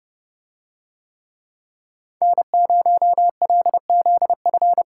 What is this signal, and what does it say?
Morse code, heard as a keyed tone, about 31 wpm, 700 Hz. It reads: N0LZF